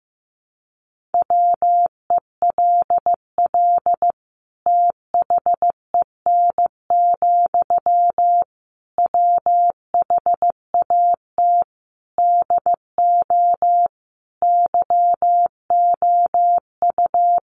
Morse code, heard as a keyed tone, about 14 wpm, 700 Hz. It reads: WELL THEN, WHAT DO YOU
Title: Sentences from Top 200 Words 15wpm